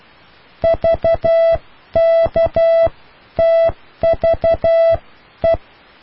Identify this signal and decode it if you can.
Morse code, heard as a keyed tone, about 12 wpm, spaced as for 10 wpm, 660 Hz. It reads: VKTVE